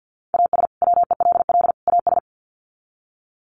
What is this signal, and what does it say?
Morse code, heard as a keyed tone, about 50 wpm, 700 Hz. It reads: AS WELL AS
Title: Sets of 3 Words 50wpm